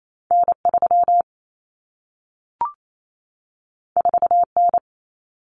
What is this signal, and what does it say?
Morse code, heard as a keyed tone, about 28 wpm, 700 Hz. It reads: N3 4N
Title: Common Cut Numbers - 2-Digit 28wpm